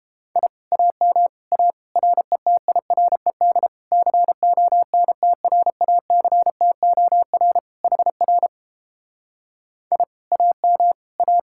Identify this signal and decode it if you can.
Morse code, heard as a keyed tone, about 31 wpm, 700 Hz. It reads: I AM A RETIRED CONTRACTOR HR I AM A